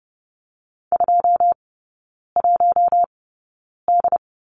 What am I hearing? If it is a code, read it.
Morse code, heard as a keyed tone, about 30 wpm, spaced as for 9 wpm, 700 Hz. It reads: 21D
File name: course / 30:10wpm / 087 Lesson 15e - Number 5 - Three Characters 30wpm